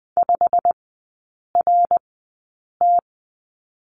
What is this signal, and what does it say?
Morse code, heard as a keyed tone, about 19 wpm, spaced as for 8 wpm, 690 Hz. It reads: 5RT
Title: 087 Lesson 15e - Number 5 - Three Characters 20wpm